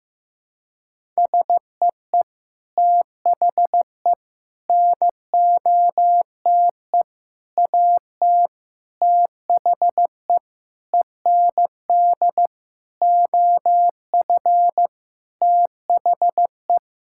Morse code, heard as a keyed tone, about 15 wpm, 700 Hz. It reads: SEE THE NOTE AT THE END OF THE
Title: Sentences from Top 1000 Words - Review Part 5 - 15wpm